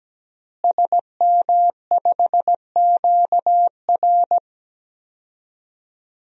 Morse code, heard as a keyed tone, about 17 wpm, 700 Hz. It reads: SM5QR
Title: DXCC-100 Call Signs - 5 Characters - Repeated 3x - 17wpm